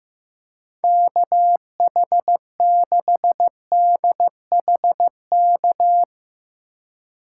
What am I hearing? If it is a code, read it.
Morse code, heard as a keyed tone, about 15 wpm, 700 Hz. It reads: KH6DHK
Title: K1USN SST QSO Practice - 15wpm